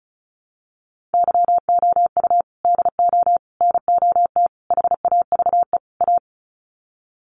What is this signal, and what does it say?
Morse code, heard as a keyed tone, about 34 wpm, 700 Hz. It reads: YOU DO NOT HAVE A
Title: Sets of 5 Words 35wpm